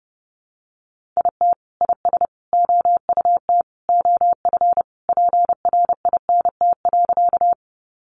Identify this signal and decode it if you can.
Morse code, heard as a keyed tone, about 30 wpm, 700 Hz. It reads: IT IS OUT OF PRINT.
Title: Sentences from Top 1400 Words 30wpm